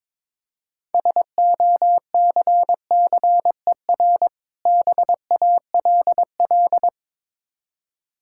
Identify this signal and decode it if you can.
Morse code, heard as a keyed tone, about 22 wpm, 690 Hz. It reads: SOCCER BALL
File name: Difficult Words with Hints - 22wpm